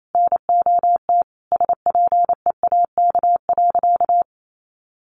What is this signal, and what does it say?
Morse code, heard as a keyed tone, about 28 wpm, 700 Hz. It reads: NOT SPEAK.
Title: Sentences from Top 1100 Words 28wpm